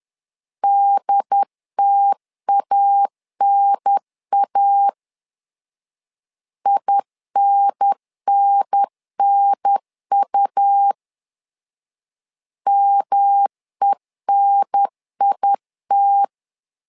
Morse code, heard as a keyed tone, about 11 wpm, 800 Hz. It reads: DTANA INNNU MENIT
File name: LEZIONI / 3 MWRO / 3 M 10wpm 1828